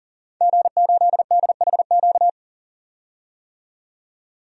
Morse code, heard as a keyed tone, about 40 wpm, 700 Hz. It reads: G8DHQ
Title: DXCC-100 Call Signs - 5 Characters - 40wpm